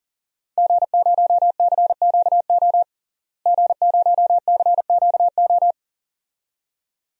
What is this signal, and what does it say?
Morse code, heard as a keyed tone, about 38 wpm, 700 Hz. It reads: G0CQO G0CQO
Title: DXCC-100 Call Signs - 5 Characters - Repeated 2x - 40wpm